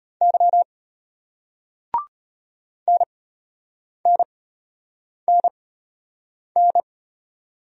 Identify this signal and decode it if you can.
Morse code, heard as a keyed tone, about 32 wpm, spaced as for 8 wpm, 700 Hz. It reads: Y NNNN